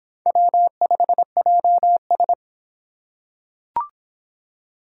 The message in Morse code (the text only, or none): W5JS